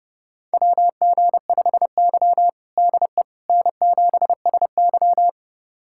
Morse code, heard as a keyed tone, about 30 wpm, 700 Hz. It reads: WG5Y DE N7SY
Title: USA Callsign DE Callsign - 4 Characters - 30wpm